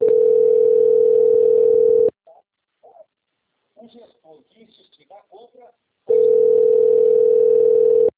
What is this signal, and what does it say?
Ringback tone (North America), 440 + 480 Hz, with speech.
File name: out- - - - - .1822